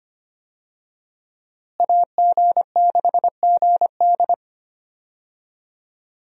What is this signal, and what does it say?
Morse code, heard as a keyed tone, about 25 wpm, 700 Hz. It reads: AG6GD
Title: USA Call Signs - 5 Characters - Repeated 2x - 25wpm